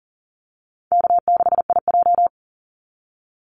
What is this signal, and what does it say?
Morse code, heard as a keyed tone, about 40 wpm, 700 Hz. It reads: K6IJ
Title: Easy Call Signs Repeated 2x - 40wpm